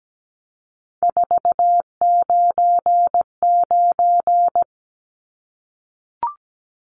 Morse code, heard as a keyed tone, about 17 wpm, 700 Hz. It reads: 499
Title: Numbers - 3-Digit 17wpm